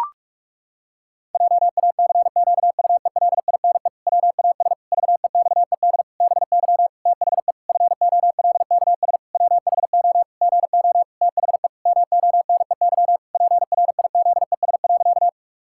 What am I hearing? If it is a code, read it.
Morse code, heard as a keyed tone, about 46 wpm, 700 Hz. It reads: JACQUELINE WAS VEXED BY THE FOLKS WHO GO THE MONEY PRIZES.
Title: Pangram Challenge v2 - 45wpm